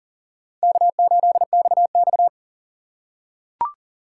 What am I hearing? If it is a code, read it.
Morse code, heard as a keyed tone, about 40 wpm, 700 Hz. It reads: K8XX
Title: USA Call Signs - 4 Characters - 2x Character Spacing - 40wpm